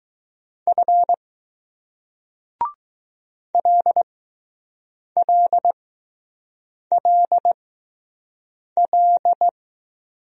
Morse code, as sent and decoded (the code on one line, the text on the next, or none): ..-. / .-.. .-.. .-.. .-..
F LLLL